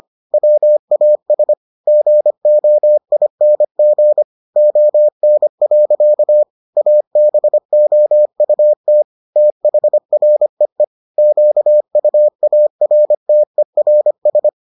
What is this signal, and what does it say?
Morse code, heard as a keyed tone, about 25 wpm, 600 Hz. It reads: WAS GOING ON. ABOUT THREE QUARTERS